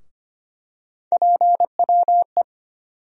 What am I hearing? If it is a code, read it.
Morse code, heard as a keyed tone, about 25 wpm, 700 Hz. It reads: PWE
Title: USA Call Sign Postfixes - 3 Letters 25wpm